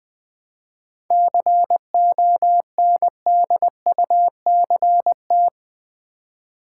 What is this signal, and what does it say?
Morse code, heard as a keyed tone, about 20 wpm, 700 Hz. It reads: CONDUCT
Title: 7 Letter Words - First 3 Letters Given - 20wpm